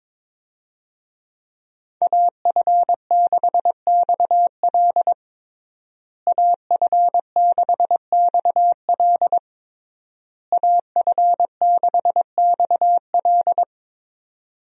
Morse code, heard as a keyed tone, about 22 wpm, 700 Hz. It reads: AF6XL AF6XL AF6XL
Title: USA Call Signs - 5 Characters - Repeated 3x - 22wpm